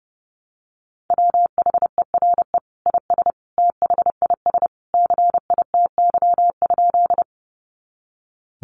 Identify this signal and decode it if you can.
Morse code, heard as a keyed tone, about 30 wpm, 700 Hz. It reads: WHERE IS THIS CITY?
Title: Sentences from Top 300 Words 30wpm